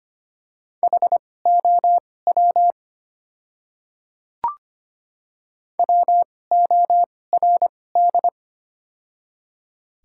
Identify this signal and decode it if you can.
Morse code, heard as a keyed tone, about 25 wpm, spaced as for 18 wpm, 700 Hz. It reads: HOW WORD
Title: 2x Character Spacing - Top 100 Words - 25wpm